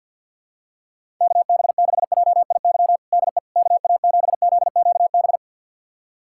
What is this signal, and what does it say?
Morse code, heard as a keyed tone, about 52 wpm, 700 Hz. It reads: KB6JIY DE KA7ZQB
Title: USA Callsign DE Callsign - 6 Characters - 50wpm